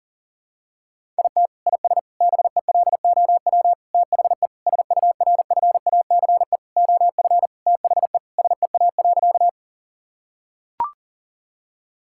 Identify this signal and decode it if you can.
Morse code, heard as a keyed tone, about 40 wpm, 700 Hz. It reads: IT IS BELOW THE SURFACE OF THE SEA.
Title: Sentences from Top 1000 Words - Review Part 5 - 40wpm